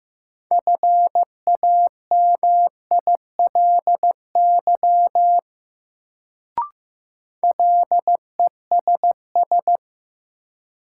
Morse code, heard as a keyed tone, about 15 wpm, 700 Hz. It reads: FAMILY LESS
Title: Sentences from Top 400 Words - New Word Drill - 15wpm